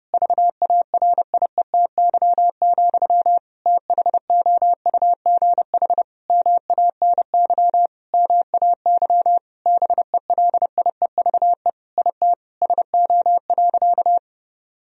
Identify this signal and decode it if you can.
Morse code, heard as a keyed tone, about 32 wpm, 700 Hz. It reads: VARIETY, THOUGH MANY MAY BELIEVE IT SO.
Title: Sentences from Top 1000 Words - Review Part 4 - 30wpm